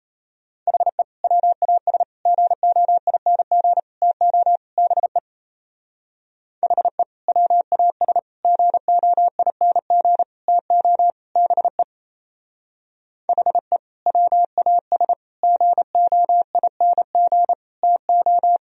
Morse code, heard as a keyed tone, about 37 wpm, 700 Hz. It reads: HE WAS GOING TO BE HE WAS GOING TO BE HE WAS GOING TO